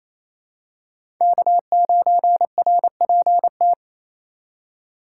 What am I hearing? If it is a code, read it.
Morse code, heard as a keyed tone, about 28 wpm, 700 Hz. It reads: K9RPT
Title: USA Call Signs - 5 Characters - 2x Character Spacing - 28wpm